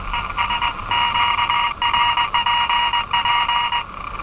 Morse code, heard as a keyed tone, about 21 wpm, 950 Hz, with noise. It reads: ES QRPP
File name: attachments / RV3GM EW1CY